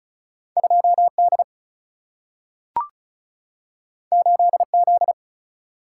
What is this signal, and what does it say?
Morse code, heard as a keyed tone, about 36 wpm, 700 Hz. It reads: 2 D 8 Z
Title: Single-Letter Single-Number 35wpm